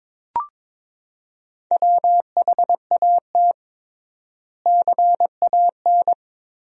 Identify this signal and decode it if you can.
Morse code, heard as a keyed tone, about 22 wpm, 700 Hz. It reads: WHAT CAN
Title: 3x Word-Spacing - Sentences from Top 100 Words - 22wpm